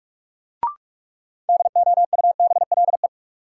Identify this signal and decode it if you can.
Morse code, heard as a keyed tone, about 46 wpm, 700 Hz. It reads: DOUBLE
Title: Sentences from Top 1400 Words - New Word Drill - 45wpm